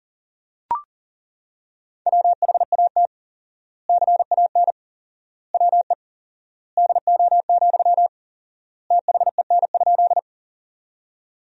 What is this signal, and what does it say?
Morse code, heard as a keyed tone, about 40 wpm, 700 Hz. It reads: WHAT CAN WE DO, THEN?
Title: 4x Word-Spacing - Sentences from Top 100 Words - 40wpm